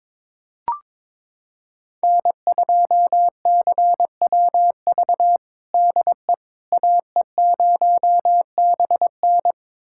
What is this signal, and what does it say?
Morse code, heard as a keyed tone, about 21 wpm, 700 Hz. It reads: N2CWV DE AE0BN
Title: USA Callsign DE Callsign - 5 Characters - 22wpm